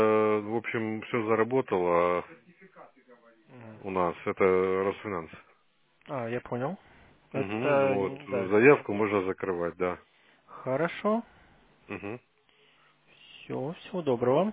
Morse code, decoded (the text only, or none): none